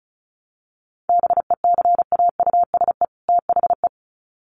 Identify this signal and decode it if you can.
Morse code, heard as a keyed tone, about 34 wpm, 700 Hz. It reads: BECAUSE THE
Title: Sets of 2 Words 35wpm